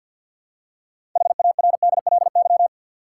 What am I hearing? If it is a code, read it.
Morse code, heard as a keyed tone, about 52 wpm, 700 Hz. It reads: HARDLY
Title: Six Letter Words - 50wpm